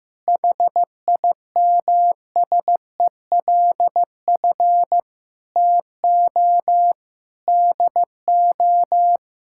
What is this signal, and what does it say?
Morse code, heard as a keyed tone, about 15 wpm, 700 Hz. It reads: HIMSELF TO DO
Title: Sentences from Top 1600 Words 15wpm